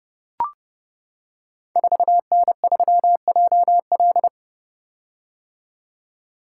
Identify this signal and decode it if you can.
Morse code, heard as a keyed tone, about 28 wpm, 700 Hz. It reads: 4N3JL
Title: Code Groups - 5 Letter:Numbers - 30wpm